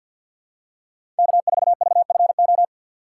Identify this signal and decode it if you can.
Morse code, heard as a keyed tone, about 53 wpm, 700 Hz. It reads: K4VFO
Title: USA Call Signs - 5 Characters - 2x Character Spacing - 50wpm